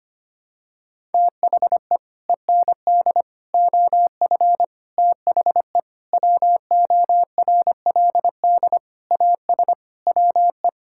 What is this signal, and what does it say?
Morse code, heard as a keyed tone, about 25 wpm, 700 Hz. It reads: THE END OF THE WORLD AS WE
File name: Sets of 9 Words - 25wpm